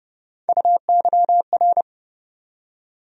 Morse code, heard as a keyed tone, about 31 wpm, 700 Hz. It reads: UYR
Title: USA Call Sign Postfixes - 3 Letters 30wpm